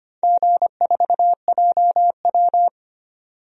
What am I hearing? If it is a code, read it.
Morse code, heard as a keyed tone, about 25 wpm, 700 Hz. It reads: G4JW
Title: DXCC-100 Call Signs - 4 Characters - 25wpm